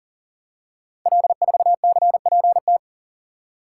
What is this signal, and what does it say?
Morse code, heard as a keyed tone, about 39 wpm, 690 Hz. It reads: L4CPT
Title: Code Groups - 5 Letter:Numbers - 40wpm